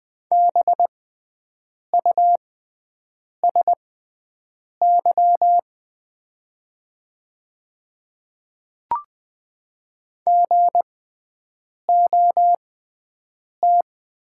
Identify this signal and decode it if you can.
Morse code, heard as a keyed tone, about 20 wpm, spaced as for 7 wpm, 700 Hz. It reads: BUSY GOT
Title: 6x Character Spacing - Top 500 Words - 20wpm